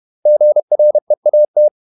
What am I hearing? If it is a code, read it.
Morse code, heard as a keyed tone, about 31 wpm, 600 Hz. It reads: GREAT
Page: Never 31wpm | CW med Gnister